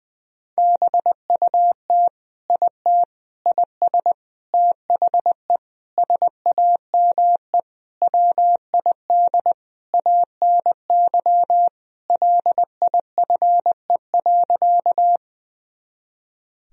Morse code, heard as a keyed tone, about 20 wpm, 700 Hz. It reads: BUT IT IS THE SAME WID ANY LIFE.
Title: Sentences from Top 300 Words and Abbreviations 20wpm